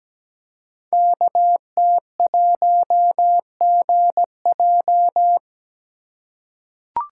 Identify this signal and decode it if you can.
Morse code, heard as a keyed tone, about 17 wpm, 700 Hz. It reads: KT1GJ
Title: Code Groups - 5 Letter:Numbers - 17wpm